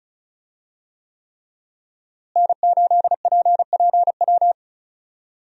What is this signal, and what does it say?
Morse code, heard as a keyed tone, about 36 wpm, 700 Hz. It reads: N8PPW
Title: USA Call Signs - 5 Characters - Repeated 3x - 35wpm